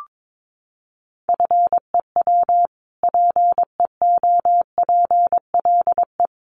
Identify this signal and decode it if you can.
Morse code, heard as a keyed tone, about 22 wpm, 700 Hz. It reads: FEW PEOPLE
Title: Sentences from Top 600 Words 22wpm